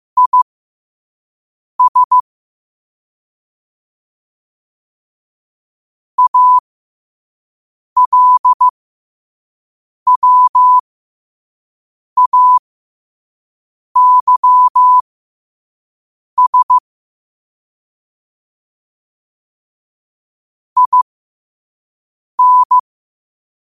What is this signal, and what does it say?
Morse code, heard as a keyed tone, about 15 wpm, spaced as for 5 wpm, 1000 Hz. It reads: IS ALWAYS IN